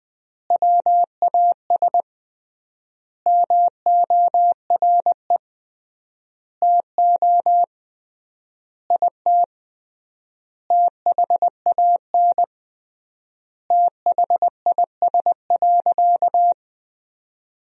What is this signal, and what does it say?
Morse code, heard as a keyed tone, about 20 wpm, 700 Hz. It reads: WAS MORE TO IT THAN THIS.
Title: 3x Word-Spacing - Sentences from Top 100 Words - 20wpm